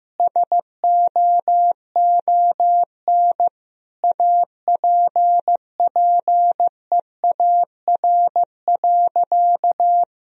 Morse code, heard as a keyed tone, about 15 wpm, 700 Hz. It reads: SOON APPEAR.
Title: Sentences from Top 1000 Words - Review Part 2 - 15wpm